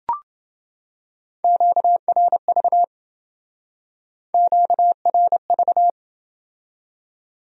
Morse code, heard as a keyed tone, about 28 wpm, 700 Hz. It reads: QRV QRV